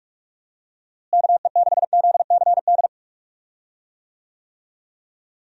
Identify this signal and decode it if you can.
Morse code, heard as a keyed tone, about 43 wpm, 700 Hz. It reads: KE6ZCD
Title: USA Call Signs - 6 Characters - Repeated 2x - 45wpm